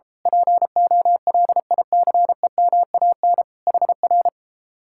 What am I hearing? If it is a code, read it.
Morse code, heard as a keyed tone, about 34 wpm, 700 Hz. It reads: POLICEMAN HR